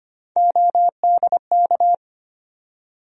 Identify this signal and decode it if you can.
Morse code, heard as a keyed tone, about 25 wpm, 700 Hz. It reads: ODK
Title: USA Call Sign Postfixes - 3 Letters 25wpm